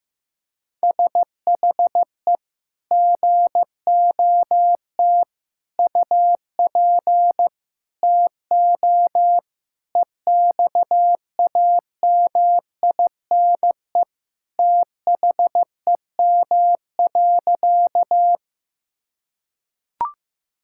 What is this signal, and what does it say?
Morse code, heard as a keyed tone, about 16 wpm, 700 Hz. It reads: SHE GOT UP TO EXAMINE THEM.
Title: Sentences from Top 1600 Words 15wpm